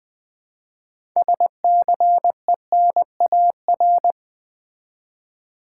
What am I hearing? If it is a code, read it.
Morse code, heard as a keyed tone, about 20 wpm, 700 Hz. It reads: SCENAR
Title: Build A Word - 3 Letters At A Time - 8-Letter Words - 20wpm